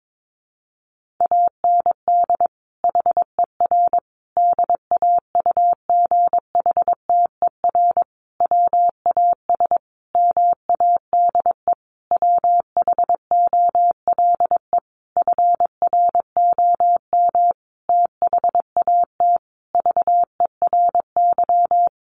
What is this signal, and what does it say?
Morse code, heard as a keyed tone, about 22 wpm, 700 Hz. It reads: AND HER DAUGHTER WAS MADE WHOLE FROM THAT VERY
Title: Sentences from Top 1500 Words 22wpm